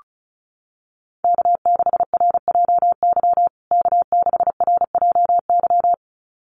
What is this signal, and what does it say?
Morse code, heard as a keyed tone, about 34 wpm, 700 Hz. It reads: K6RJY K6RJY
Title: USA Call Signs Repeated 2x 35wpm